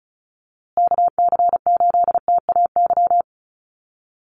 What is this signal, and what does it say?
Morse code, heard as a keyed tone, about 34 wpm, 700 Hz. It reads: KC8TAY
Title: USA Call Signs Repeated 2x 35wpm